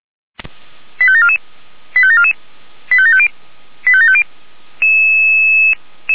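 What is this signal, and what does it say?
Five-tone selective call (ZVEI), heard as quick runs of five tones, tones of about 70 ms. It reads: 85744 85744 85750 85750
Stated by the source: Received by SDR.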